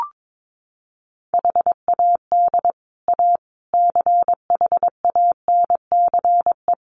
Morse code, heard as a keyed tone, about 22 wpm, 700 Hz. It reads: HAD A CHANCE
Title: Sets of 4 Words 22wpm